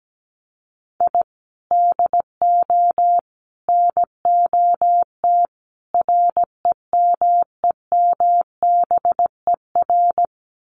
Morse code, heard as a keyed tone, about 17 wpm, 700 Hz. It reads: I DO NOT REMEMBER
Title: Sets of 4 Words 17wpm